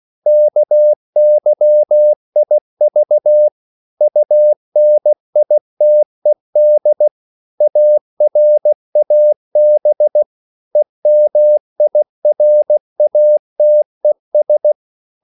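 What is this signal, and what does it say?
Morse code, heard as a keyed tone, about 16 wpm, 600 Hz. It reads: KYIV UNITED ARAB EMIRATES
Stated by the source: CW practice of national capitals at 16 words per minute, from ON6ZQ.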